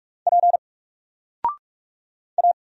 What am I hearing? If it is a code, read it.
Morse code, heard as a keyed tone, about 44 wpm, 700 Hz. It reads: P A
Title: Single Letter-Number 45wpm